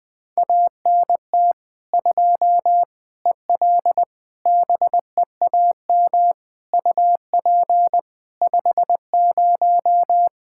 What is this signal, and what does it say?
Morse code, heard as a keyed tone, about 20 wpm, 700 Hz. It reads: ANT 2 EL BEAM UP 50
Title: Practice QSO - Exchange 2 - Info - 20wpm